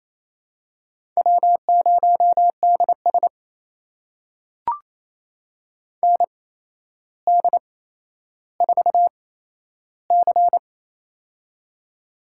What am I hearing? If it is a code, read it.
Morse code, heard as a keyed tone, about 28 wpm, spaced as for 12 wpm, 700 Hz. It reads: W0DS ND4C
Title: USA Call Signs - 4 Characters - 8x Character Spacing - 28wpm